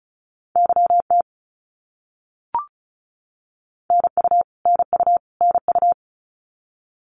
Morse code, heard as a keyed tone, about 34 wpm, 700 Hz. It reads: YT NU NU NU
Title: Canadian Provinces and Territories - Repeated 3x - 35wpm